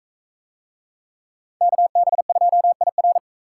Morse code, heard as a keyed tone, about 41 wpm, 700 Hz. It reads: KB2IR